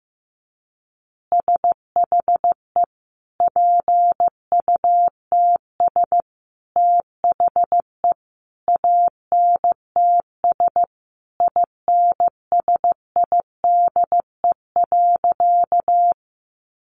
Morse code, heard as a keyed tone, about 15 wpm, 700 Hz. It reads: SHE PUTS THE ANTS INSIDE.
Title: Easy Phrases 15wpm